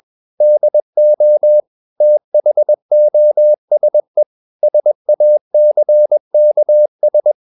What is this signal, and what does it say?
Morse code, heard as a keyed tone, about 21 wpm, 600 Hz. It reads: DO THOSE SACKS
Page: CW med Gnister